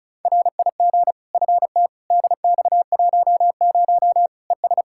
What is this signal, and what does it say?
Morse code, heard as a keyed tone, about 36 wpm, 700 Hz. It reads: RIG FT DX10 ES
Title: K1USN SST QSO Practice 35wpm